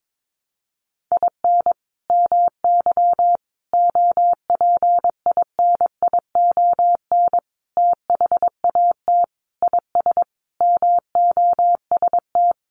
Morse code, heard as a keyed tone, about 22 wpm, 700 Hz. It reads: IN MY OPINION THAT IS MOST
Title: Sentences from Top 1200 Words 22wpm